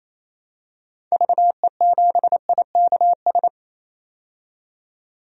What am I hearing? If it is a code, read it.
Morse code, heard as a keyed tone, about 28 wpm, 700 Hz. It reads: VE7IKS
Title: DXCC-100 Call Signs - 6 Characters - 28wpm